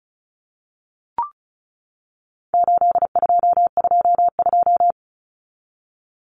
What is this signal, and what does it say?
Morse code, heard as a keyed tone, about 34 wpm, 700 Hz. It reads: 8222
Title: Common Cut Numbers - 4-Digit 35wpm